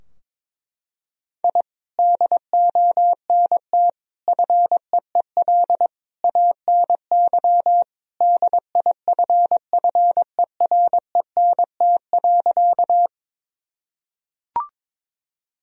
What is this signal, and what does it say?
Morse code, heard as a keyed tone, about 22 wpm, 700 Hz. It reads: I DONT FEEL ANY DIFFERENT.
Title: Sentences from Top 1000 Words - Review Part 3 - 22wpm